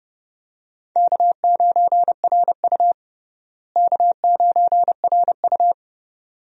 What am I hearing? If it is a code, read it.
Morse code, heard as a keyed tone, about 32 wpm, 700 Hz. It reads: K9RU K9RU
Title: USA Call Signs - 4 Characters - Repeated 2x - 30wpm